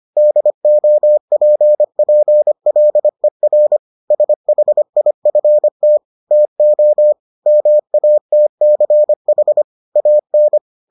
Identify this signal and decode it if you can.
Morse code, heard as a keyed tone, about 25 wpm, 600 Hz. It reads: DOPPLER SHIFT TO MATCH AN